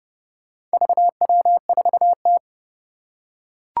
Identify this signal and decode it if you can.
Morse code, heard as a keyed tone, about 30 wpm, 700 Hz. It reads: VW4T
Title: Four Challenging Letter-Number Combinations 30wpm